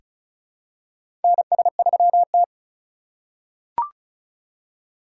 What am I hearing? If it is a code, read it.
Morse code, heard as a keyed tone, about 34 wpm, 700 Hz. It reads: NS3T
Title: USA Call Signs - 3x Character Spacing - 35wpm